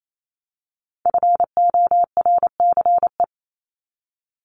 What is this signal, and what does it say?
Morse code, heard as a keyed tone, about 28 wpm, 700 Hz. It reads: FORCE
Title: ED Suffix - Top 100 Words with Base Words 28wpm